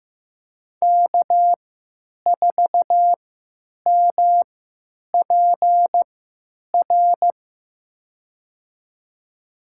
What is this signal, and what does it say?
Morse code, heard as a keyed tone, about 15 wpm, spaced as for 9 wpm, 700 Hz. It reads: K4MPR
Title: USA Call Signs - 5 Characters - 3x Character Spacing - 15wpm